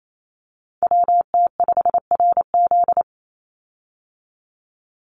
Morse code, heard as a keyed tone, about 28 wpm, 700 Hz. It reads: WT5RZ
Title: Call Signs Repeated 3x - 28wpm